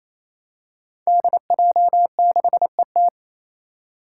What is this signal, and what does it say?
Morse code, heard as a keyed tone, about 28 wpm, 700 Hz. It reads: DJ6ET
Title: Five Challenging Letter-Number Combinations 28wpm